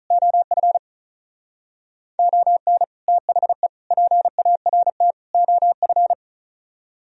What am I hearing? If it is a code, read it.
Morse code, heard as a keyed tone, about 43 wpm, 700 Hz. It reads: OF ON THE PART OF
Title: Speed-Racing - Sets of 4 Words 35wpm